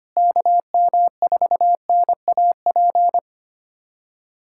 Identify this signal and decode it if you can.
Morse code, heard as a keyed tone, about 25 wpm, 700 Hz. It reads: KM4NAP
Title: K1USN SST QSO Practice - 25wpm